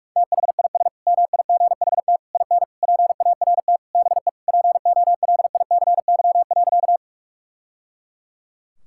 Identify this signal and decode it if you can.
Morse code, heard as a keyed tone, about 43 wpm, 700 Hz. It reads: THIS MIGHT IN PART BE POLICY.
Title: Sentences from Top 1000 Words - Review Part 5 - 45wpm